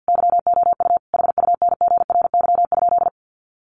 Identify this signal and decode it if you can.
Morse code, heard as a keyed tone, about 50 wpm, 700 Hz. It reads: YOU HUNGRY?
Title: Sentences from Top 1400 Words 50wpm